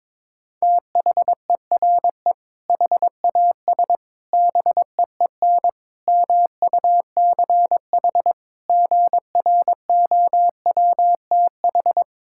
Morse code, heard as a keyed tone, about 22 wpm, 700 Hz. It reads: THERE HAS BEEN MUCH GROWTH
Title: Sentences from Top 1000 Words - Review Part 5 - 22wpm